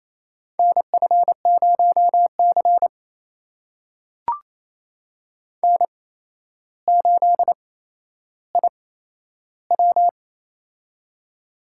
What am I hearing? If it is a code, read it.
Morse code, heard as a keyed tone, about 28 wpm, spaced as for 12 wpm, 700 Hz. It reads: NF0C N8IW
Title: USA Call Signs - 4 Characters - 8x Character Spacing - 28wpm